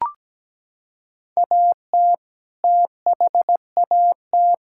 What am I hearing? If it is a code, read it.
Morse code, heard as a keyed tone, about 17 wpm, 700 Hz. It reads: AT THAT
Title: Sentences from Top 1000 Words - Review Part 5 - 17wpm